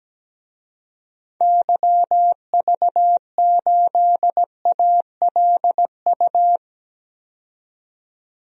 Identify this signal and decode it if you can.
Morse code, heard as a keyed tone, about 17 wpm, 700 Hz. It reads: YV8ALU
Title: DXCC-100 Call Signs - 6 Characters - 17wpm